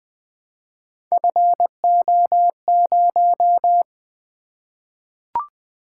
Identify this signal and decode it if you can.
Morse code, heard as a keyed tone, about 20 wpm, 700 Hz. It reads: FO0
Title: Three Letter-Number Combinations - 20wpm